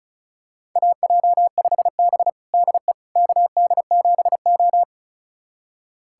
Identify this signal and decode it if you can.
Morse code, heard as a keyed tone, about 36 wpm, 700 Hz. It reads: AJ5B DE KD7O
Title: USA Callsign DE Callsign - 4 Characters - 35wpm